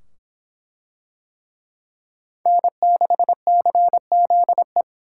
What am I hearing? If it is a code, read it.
Morse code, heard as a keyed tone, about 24 wpm, 700 Hz. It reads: N6CZE